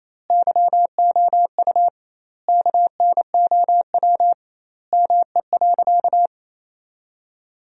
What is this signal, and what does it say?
Morse code, heard as a keyed tone, about 28 wpm, 700 Hz. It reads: YOU KNOW ME.
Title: 2x Word-Spacing - Sentences from Top 200 Words - 28wpm